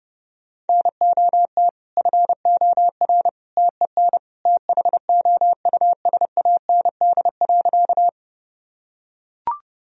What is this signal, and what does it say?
Morse code, heard as a keyed tone, about 30 wpm, 700 Hz. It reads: NOT FOR TEN THOUSAND.
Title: Sentences from Top 1000 Words - Review Part 3 - 30wpm